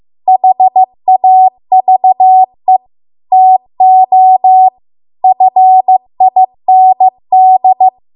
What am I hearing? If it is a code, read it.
Morse code, heard as a keyed tone, about 15 wpm, 750 Hz. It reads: HAVE TO FIND
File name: Archive / 15 WPM / 15WPM